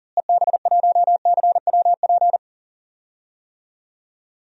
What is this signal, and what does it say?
Morse code, heard as a keyed tone, about 40 wpm, 700 Hz. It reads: EB1CWP
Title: DXCC-100 Call Signs - 6 Characters - Repeated 3x - 40wpm